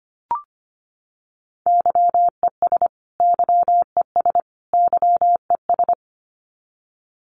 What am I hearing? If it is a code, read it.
Morse code, heard as a keyed tone, about 25 wpm, 700 Hz. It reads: YES YES YES
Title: Top 500 Words Repeated 25wpm